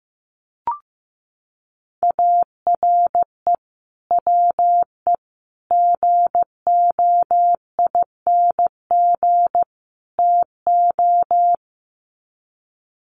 Morse code, heard as a keyed tone, about 15 wpm, 700 Hz. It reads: ARE WE GOING TO
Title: Sets of 4 Words 15wpm